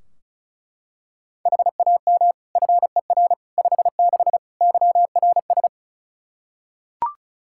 Morse code, heard as a keyed tone, about 36 wpm, 700 Hz. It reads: HAM FER 56 YRS
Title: Practice QSO - Exchange 3 - More Info - 35wpm